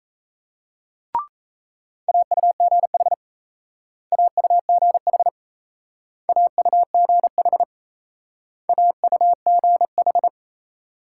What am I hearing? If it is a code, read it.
Morse code, heard as a keyed tone, about 35 wpm, 700 Hz. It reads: AUGH AUGH AUGH AUGH
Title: Speed-Racing Multi-letter Phonemes - 28wpm